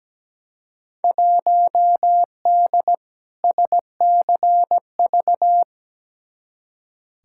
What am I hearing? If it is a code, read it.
Morse code, heard as a keyed tone, about 17 wpm, 700 Hz. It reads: 1D SCV
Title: Field Day Exchanges Encore - 17wpm